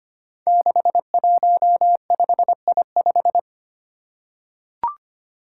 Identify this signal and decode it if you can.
Morse code, heard as a keyed tone, about 25 wpm, 700 Hz. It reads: 615I5
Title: Five Challenging Letter-Number Combinations 25wpm